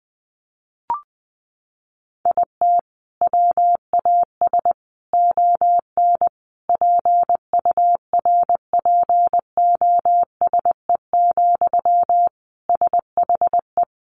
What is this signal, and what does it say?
Morse code, heard as a keyed tone, about 20 wpm, 700 Hz. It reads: IT WAS ON PURPOSE, SHE
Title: Sentences from Top 800 Words 20wpm